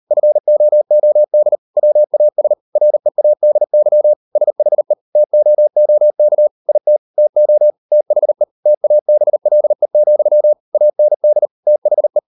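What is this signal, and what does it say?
Morse code, heard as a keyed tone, about 38 wpm, 600 Hz. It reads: FOOD WAS READY SHE TOOK IT TO THE TABLE, AND THE
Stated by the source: Morse code practice at 38 WPM, sending Ken Follett's novel Never in English, from CW med Gnister.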